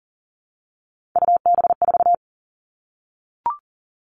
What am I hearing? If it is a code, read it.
Morse code, heard as a keyed tone, about 37 wpm, 700 Hz. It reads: UB4
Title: Three Challenging Letter-Number Combinations 40wpm